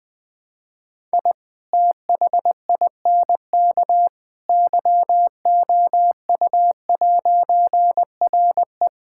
Morse code, heard as a keyed tone, about 20 wpm, 700 Hz. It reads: I THINK YOU'RE
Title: All-Age Appropriate Jokes - 20wpm